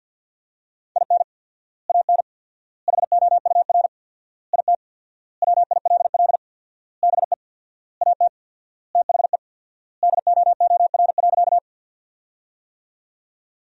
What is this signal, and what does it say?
Morse code, heard as a keyed tone, about 50 wpm, 700 Hz. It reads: IN AN HOUR IT WILL BE AT THE DOOR.
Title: 4x Word-Spacing - Sentences from Top 400 Words - 50wpm